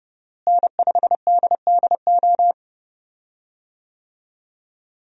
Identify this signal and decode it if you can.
Morse code, heard as a keyed tone, about 30 wpm, 700 Hz. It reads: N5DDO
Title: USA Call Signs - 5 Characters - Repeated 3x - 30wpm